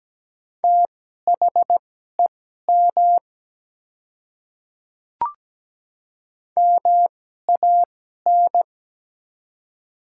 Morse code, heard as a keyed tone, about 17 wpm, spaced as for 12 wpm, 700 Hz. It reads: THEM MAN
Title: 2x Character Spacing - Top 200 Words - 17wpm